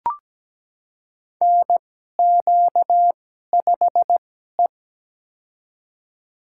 Morse code, heard as a keyed tone, about 17 wpm, spaced as for 12 wpm, 700 Hz. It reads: NQ5E
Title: USA Call Signs - 2x Character Spacing - 17wpm